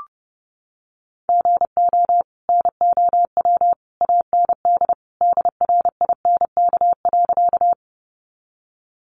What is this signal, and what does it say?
Morse code, heard as a keyed tone, about 30 wpm, 700 Hz. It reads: GO NOW AND DRINK.
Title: Sentences from Top 1500 Words 30wpm